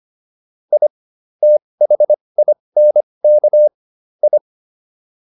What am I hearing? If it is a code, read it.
Morse code, heard as a keyed tone, about 25 wpm, spaced as for 20 wpm, 600 Hz. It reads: I THINK I